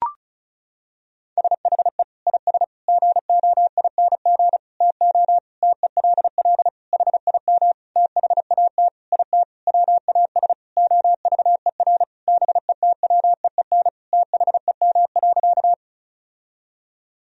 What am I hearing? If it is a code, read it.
Morse code, heard as a keyed tone, about 36 wpm, 700 Hz. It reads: SHE IS GOING TO TELL HIM THAT IT WAS OVER BETWEEN THEM.
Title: Sentences from Top 1000 Words - Review Part 1 - 35wpm